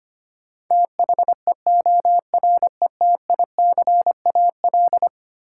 Morse code, heard as a keyed tone, about 25 wpm, 700 Hz. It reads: THEORETICAL
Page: morse-code-ninja-practice-25wpm